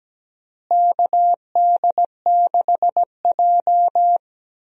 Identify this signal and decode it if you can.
Morse code, heard as a keyed tone, about 17 wpm, 700 Hz. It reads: KD6J
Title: USA Call Signs - 4 Characters - Repeated 3x - 17wpm